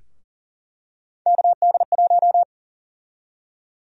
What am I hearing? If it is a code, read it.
Morse code, heard as a keyed tone, about 40 wpm, 700 Hz. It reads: KD1
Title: USA Call Sign Prefixes - 40wpm